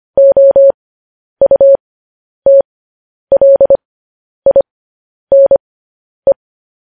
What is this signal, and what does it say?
Morse code, heard as a keyed tone, about 25 wpm, spaced as for 10 wpm, 550 Hz. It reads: OUTLINE
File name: podcast / News headlines-1x-25x10